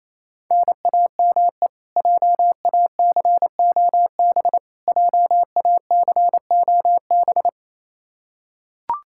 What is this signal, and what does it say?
Morse code, heard as a keyed tone, about 28 wpm, 700 Hz. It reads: NAME JACOB JACOB
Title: K1USN SST QSO Practice 28wpm